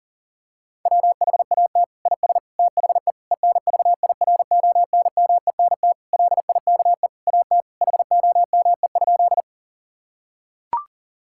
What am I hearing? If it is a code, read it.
Morse code, heard as a keyed tone, about 40 wpm, 700 Hz. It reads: WHAT IS THE ENVIRONMENT LIKE AT HOME?
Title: Sentences from Top 1000 Words - Review Part 5 - 40wpm